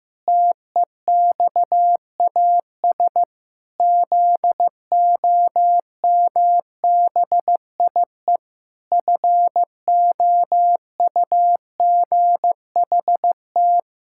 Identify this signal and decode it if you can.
Morse code, heard as a keyed tone, about 15 wpm, 700 Hz. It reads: TEXAS ZOMBIE FOUGHT
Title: Pangram Challenge v2 - 15wpm